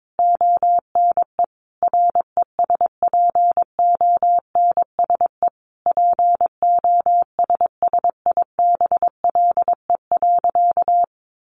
Morse code, heard as a keyed tone, about 22 wpm, 700 Hz. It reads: ONE RESPONSE POSSIBLE.
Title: Sentences from Top 1400 Words 22wpm